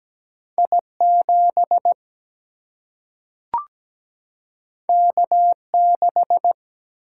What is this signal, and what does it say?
Morse code, heard as a keyed tone, about 17 wpm, 700 Hz. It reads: I7 K6
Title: Single-Letter Single-Number 17wpm